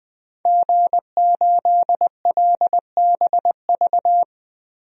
Morse code, heard as a keyed tone, about 20 wpm, 700 Hz. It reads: G8LBV
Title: DXCC-100 Call Signs - 5 Characters - 20wpm